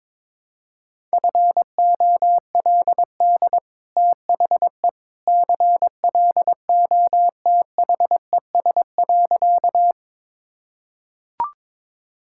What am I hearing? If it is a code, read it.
Morse code, heard as a keyed tone, about 22 wpm, 700 Hz. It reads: FOLD THE CLOTHES.
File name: practice / 22wpm / Easy Sentences with Hints - 22wpm